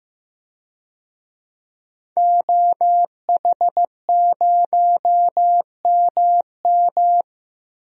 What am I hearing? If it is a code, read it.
Morse code, heard as a keyed tone, about 15 wpm, 700 Hz. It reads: OH0MM
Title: DXCC-100 Call Signs - 5 Characters - 15wpm